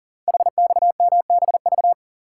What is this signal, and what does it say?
Morse code, heard as a keyed tone, about 39 wpm, 700 Hz. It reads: HXMBV
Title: Code Groups - 5 Letters - 40wpm